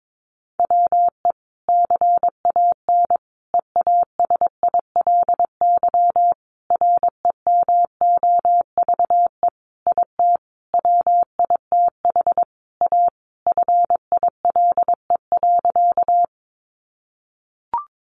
Morse code, heard as a keyed tone, about 22 wpm, 700 Hz. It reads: WE CAN EASILY REMOVE IT WITH A FILE.
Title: Sentences from Top 1200 Words 22wpm